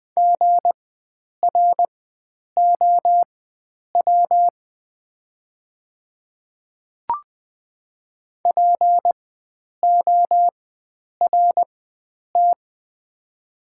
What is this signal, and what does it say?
Morse code, heard as a keyed tone, about 20 wpm, spaced as for 9 wpm, 700 Hz. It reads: GROW PORT
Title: 4x Character Spacing - Top 300 Words - 20wpm